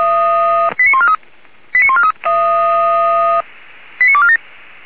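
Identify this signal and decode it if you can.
Five-tone selective call (ZVEI), heard as quick runs of five tones, tones of about 80 ms. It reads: 89143 89143 89247
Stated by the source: ZVEI five-tone selective call received by SDR.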